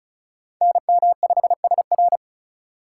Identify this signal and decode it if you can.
Morse code, heard as a keyed tone, about 45 wpm, 700 Hz. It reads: NM5SR